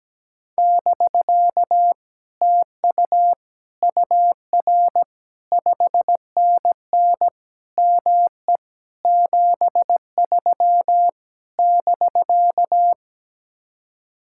Morse code, heard as a keyed tone, about 17 wpm, 700 Hz. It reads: <BK> TU UR 5NN ME 73 <BK>
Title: Speed-Racing Common QSO Phrases - 17wpm